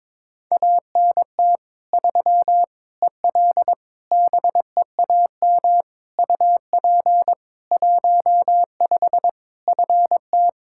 Morse code, heard as a keyed tone, about 21 wpm, 700 Hz. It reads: ANT 3 EL BEAM UP 15 FT
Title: K1USN SST QSO Practice 22wpm